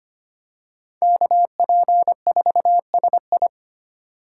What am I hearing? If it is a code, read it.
Morse code, heard as a keyed tone, about 25 wpm, 700 Hz. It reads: KP4SI
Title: USA Call Signs - 5 Characters - 4x Character Spacing - 25wpm